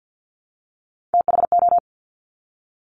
Morse code, heard as a keyed tone, about 50 wpm, 700 Hz. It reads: THO
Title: Top Three Letter Combinations in Words 50wpm